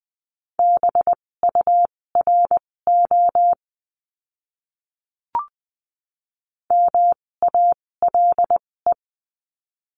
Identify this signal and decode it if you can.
Morse code, heard as a keyed tone, about 20 wpm, spaced as for 16 wpm, 700 Hz. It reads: BURO MALE
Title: 256 Lesson 42a - Words 12-20wpm